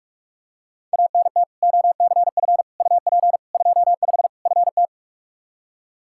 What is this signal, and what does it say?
Morse code, heard as a keyed tone, about 45 wpm, 700 Hz. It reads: ANT OCF UP 25 FT
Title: Practice QSO - Exchange 2 - Info - 45wpm